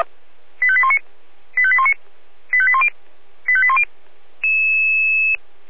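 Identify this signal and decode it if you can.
Five-tone selective call (ZVEI), heard as quick runs of five tones, tones of about 70 ms. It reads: 85819 85819 85810 85810
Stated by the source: ZVEI five-tone call received by SDR.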